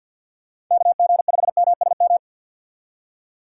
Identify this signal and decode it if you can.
Morse code, heard as a keyed tone, about 47 wpm, 700 Hz. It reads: KG5MSM
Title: USA Call Signs - 2x Character Spacing - 50wpm